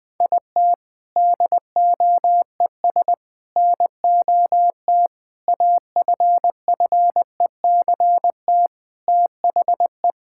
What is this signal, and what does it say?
Morse code, heard as a keyed tone, about 20 wpm, 700 Hz. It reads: IT DOES NOT AFFECT THE
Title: Sentences from Top 1000 Words - Review Part 6 - 20wpm